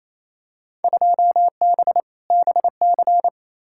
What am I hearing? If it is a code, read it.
Morse code, heard as a keyed tone, about 28 wpm, 700 Hz. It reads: 2B BC
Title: Field Day Exchanges Encore - 28wpm